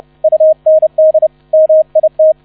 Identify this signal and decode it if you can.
Morse code, heard as a keyed tone, about 30 wpm, 630 Hz. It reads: UND MIT